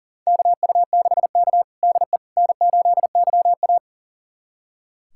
Morse code, heard as a keyed tone, about 40 wpm, 700 Hz. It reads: KU6K DE N8YA
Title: USA Callsign DE Callsign - 4 Characters - 40wpm